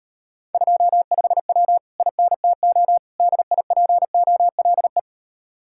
Morse code, heard as a keyed tone, about 32 wpm, 700 Hz. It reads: 25W INTO DIPOLE